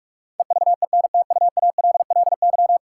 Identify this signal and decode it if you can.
Morse code, heard as a keyed tone, about 46 wpm, 700 Hz. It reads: EVENTUALLY
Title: Sentences from Top 1000 Words - Review Part 6 - New Word Drill - 45wpm